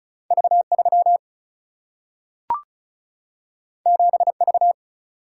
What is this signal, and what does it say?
Morse code, heard as a keyed tone, about 36 wpm, 700 Hz. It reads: V3 7V
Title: Single-Letter Single-Number 35wpm